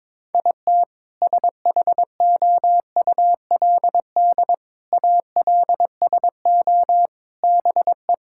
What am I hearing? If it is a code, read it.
Morse code, heard as a keyed tone, about 22 wpm, 700 Hz. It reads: IT SHOULD ALSO BE
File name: Sentences with 7-Word Phrases - 22wpm